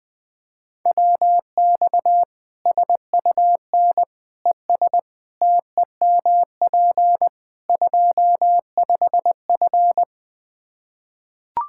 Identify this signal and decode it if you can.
Morse code, heard as a keyed tone, about 20 wpm, 700 Hz. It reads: WX SUN ES TEMP 25F
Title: Practice QSO - Exchange 2 - Info - 20wpm